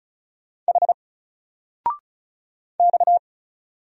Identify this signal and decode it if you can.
Morse code, heard as a keyed tone, about 35 wpm, 700 Hz. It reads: H X
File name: Single-Letter-Number-35wpm